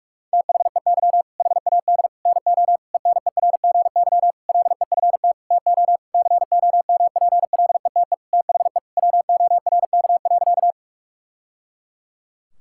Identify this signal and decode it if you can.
Morse code, heard as a keyed tone, about 41 wpm, 690 Hz. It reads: THEY HAD NO ENERGY LEFT TO COMPLETE THE WORK.
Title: Sentences from Top 1000 Words - Review Part 6 - 45wpm